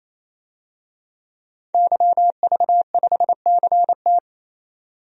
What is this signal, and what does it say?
Morse code, heard as a keyed tone, about 28 wpm, 700 Hz. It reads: YV5CT
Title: DXCC-100 Call Signs - 5 Characters - 28wpm